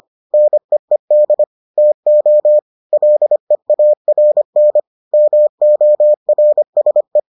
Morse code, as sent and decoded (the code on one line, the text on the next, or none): -. . -.. / - --- / .-.. .- .-. -. / -- --- .-. ... .
NEED TO LEARN MORSE